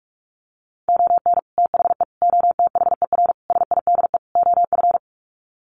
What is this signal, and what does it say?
Morse code, heard as a keyed tone, about 44 wpm, 700 Hz. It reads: ON THE OTHER SIDE OF
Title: Sets of 5 Words 45wpm